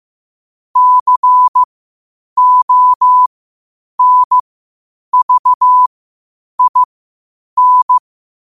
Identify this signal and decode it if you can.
Morse code, heard as a keyed tone, about 15 wpm, spaced as for 8 wpm, 1000 Hz. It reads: CONVIN
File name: qotdfiles / 8wpm